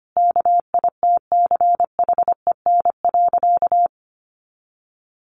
Morse code, heard as a keyed tone, about 25 wpm, 700 Hz. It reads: KITCHEN.
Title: Sentences from Top 1400 Words 25wpm